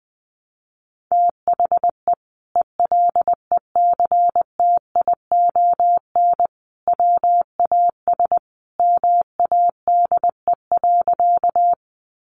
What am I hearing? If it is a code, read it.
Morse code, heard as a keyed tone, about 20 wpm, 700 Hz. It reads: THE ELECTION WAS MADE.
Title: Sentences from Top 1500 Words 20wpm